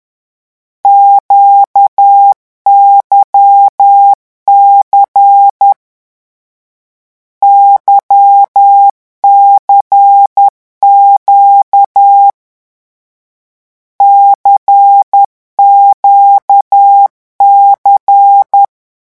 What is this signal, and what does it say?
Morse code, heard as a keyed tone, about 10 wpm, 800 Hz. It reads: QYC YCQ CQC
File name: morse mp3 / lecon2-vitesse 10